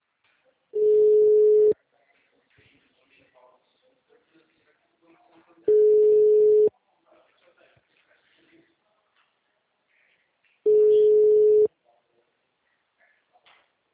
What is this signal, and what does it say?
Ringback tone (Europe, 425 Hz), with speech.